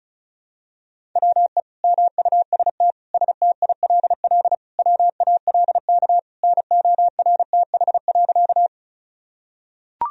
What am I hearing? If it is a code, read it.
Morse code, heard as a keyed tone, about 36 wpm, 700 Hz. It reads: WE MUST STILL WALK NORTH.
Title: Sentences from Top 1000 Words - Review Part 1 - 35wpm